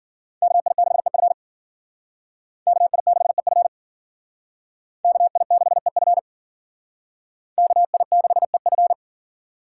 Morse code, heard as a keyed tone, about 53 wpm, 700 Hz. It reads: KI6EF KI6EF KI6EF KI6EF